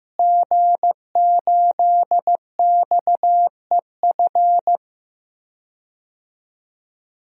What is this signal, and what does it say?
Morse code, heard as a keyed tone, about 16 wpm, 700 Hz. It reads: G8XEF
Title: DXCC-100 Call Signs - 5 Characters - Repeated 2x - 15wpm